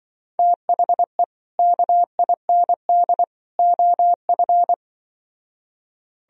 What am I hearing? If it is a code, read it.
Morse code, heard as a keyed tone, about 24 wpm, 700 Hz. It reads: THE KIND OF
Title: Speed-Racing - Sets of 3 Words 20wpm